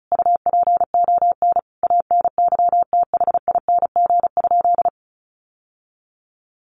Morse code, heard as a keyed tone, about 34 wpm, 700 Hz. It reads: UPON ANYTHING?
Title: Sentences from Top 1200 Words 35wpm